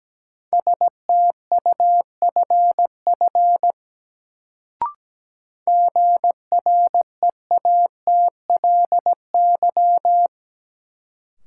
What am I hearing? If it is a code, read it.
Morse code, heard as a keyed tone, about 17 wpm, 700 Hz. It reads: STUFF GREATLY
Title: Sentences from Top 1400 Words - New Word Drill Alt - 17wpm